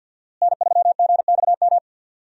Morse code, heard as a keyed tone, about 48 wpm, 700 Hz. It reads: N3GXM
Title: USA Call Signs - 5 Characters - Repeated 2x - 50wpm